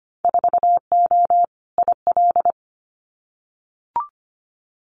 Morse code, heard as a keyed tone, about 25 wpm, 700 Hz. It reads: 4O IL
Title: Winter Field Day Exchanges 25wpm